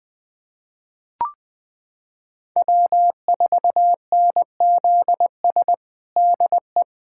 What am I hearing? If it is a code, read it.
Morse code, heard as a keyed tone, about 20 wpm, 700 Hz. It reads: W4NZS DE
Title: USA Callsign DE Callsign - 5 Characters - 20wpm